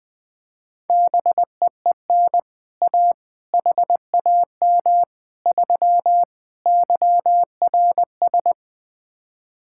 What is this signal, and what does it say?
Morse code, heard as a keyed tone, about 20 wpm, 700 Hz. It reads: BEEN A HAM 3 YRS
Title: Speed-Racing Common QSO Phrases Encore - 20wpm